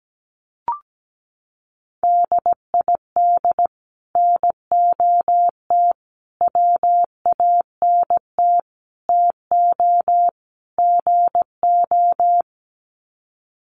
Morse code, heard as a keyed tone, about 17 wpm, 700 Hz. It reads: DID NOT WANT TO GO
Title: Sets of 5 Words 17wpm